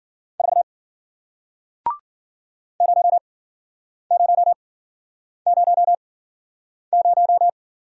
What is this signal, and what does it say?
Morse code, heard as a keyed tone, about 54 wpm, spaced as for 9 wpm, 700 Hz. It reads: 4 0000